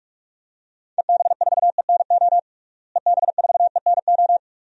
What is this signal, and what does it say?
Morse code, heard as a keyed tone, about 42 wpm, 700 Hz. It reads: EB4ENO EB4ENO
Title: DXCC-100 Call Signs - 6 Characters - Repeated 2x - 45wpm